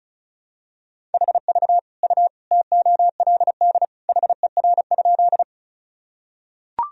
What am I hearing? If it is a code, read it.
Morse code, heard as a keyed tone, about 34 wpm, 700 Hz. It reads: HV U TOLD HER?
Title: Sentences from Top 400 Words and Abbreviations 35wpm